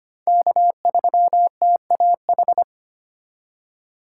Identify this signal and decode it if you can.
Morse code, heard as a keyed tone, about 25 wpm, 700 Hz. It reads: K3TAH
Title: USA Call Signs - 5 Characters - 2x Character Spacing - 25wpm